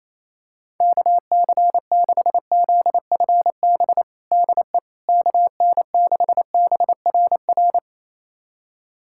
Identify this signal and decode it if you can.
Morse code, heard as a keyed tone, about 28 wpm, 700 Hz. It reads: KC6ZFB DE KN6BRR
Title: USA Callsign DE Callsign 28wpm